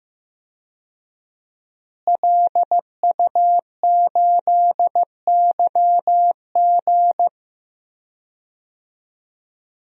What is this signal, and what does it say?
Morse code, heard as a keyed tone, about 15 wpm, 700 Hz. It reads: LU8YG
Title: DXCC-100 Call Signs - 5 Characters - Repeated 3x - 15wpm